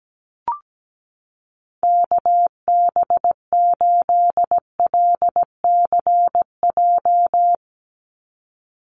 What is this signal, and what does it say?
Morse code, heard as a keyed tone, about 17 wpm, 700 Hz. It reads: KB8LCJ
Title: USA Call Signs 17wpm